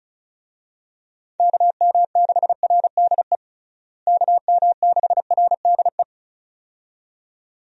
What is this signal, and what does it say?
Morse code, heard as a keyed tone, about 36 wpm, 700 Hz. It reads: KM6RDE KM6RDE
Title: USA Call Signs - 6 Characters - Repeated 2x - 35wpm